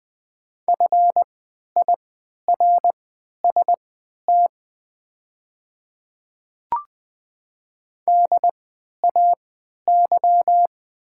Morse code, heard as a keyed tone, about 20 wpm, spaced as for 11 wpm, 700 Hz. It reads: FIRST DAY
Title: 3x Character Spacing - Top 100 Words - 20wpm